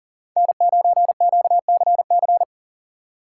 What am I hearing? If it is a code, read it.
Morse code, heard as a keyed tone, about 50 wpm, 700 Hz. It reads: N9QCC
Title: Speed Racing - USA 1x3 Call Signs 40wpm